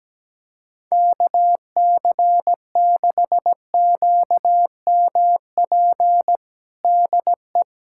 Morse code, heard as a keyed tone, about 17 wpm, 700 Hz. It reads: KC6QMP DE
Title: USA Callsign DE Callsign 17wpm